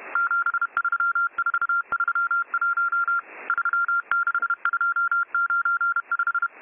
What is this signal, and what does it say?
Morse code, heard as a keyed tone, about 31 wpm, 1390 Hz. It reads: 73431 36295